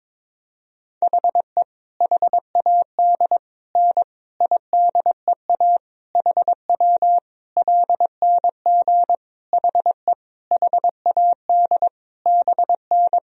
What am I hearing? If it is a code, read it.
Morse code, heard as a keyed tone, about 22 wpm, 700 Hz. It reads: HE HAD N IDEA HW LNG HE HAD BN
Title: Sentences from Top 300 Words and Abbreviations 22wpm